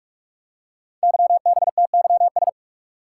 Morse code, heard as a keyed tone, about 45 wpm, 700 Hz. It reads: YBTYS
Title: Code Groups - 5 Letters - 45wpm